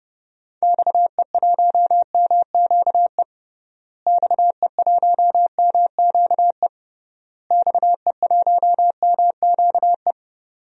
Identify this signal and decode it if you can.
Morse code, heard as a keyed tone, about 30 wpm, 700 Hz. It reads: XE1MQE XE1MQE XE1MQE
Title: DXCC-100 Call Signs - 6 Characters - Repeated 3x - 30wpm